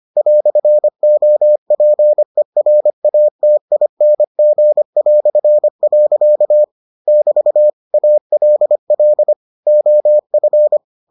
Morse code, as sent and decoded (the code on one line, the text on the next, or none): .-..-. --- .--. .-. .- - .. -. --. .-..-. .-.-.- / -...- / .- .-.. .-.. / --- ..-.
"OPERATING". <BT> ALL OF